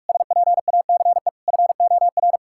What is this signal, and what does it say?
Morse code, heard as a keyed tone, about 41 wpm, 700 Hz. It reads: SPACE FOR
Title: Sentences from Top 1000 Words - Review Part 5 - 45wpm